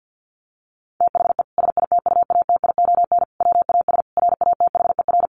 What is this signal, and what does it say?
Morse code, heard as a keyed tone, about 50 wpm, 700 Hz. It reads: THE SITUATION WAS RATHER
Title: Sentences from Top 1300 Words 50wpm